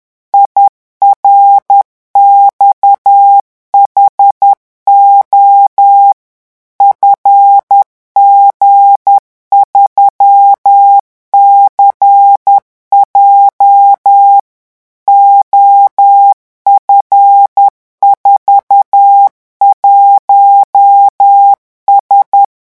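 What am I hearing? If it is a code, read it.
Morse code, heard as a keyed tone, about 10 wpm, 800 Hz. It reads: IRXHO FG3CJ OF41S
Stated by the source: Morse revision exercise for French radio amateurs at 10 words per minute.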